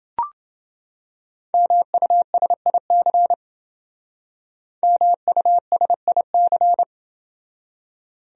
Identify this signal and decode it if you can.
Morse code, heard as a keyed tone, about 28 wpm, 700 Hz. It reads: MUSIC MUSIC